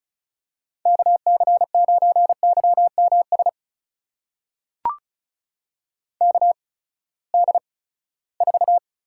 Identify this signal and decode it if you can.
Morse code, heard as a keyed tone, about 36 wpm, spaced as for 20 wpm, 700 Hz. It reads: KC9YMS KD4
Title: USA Call Signs - 8x Character Spacing - 35wpm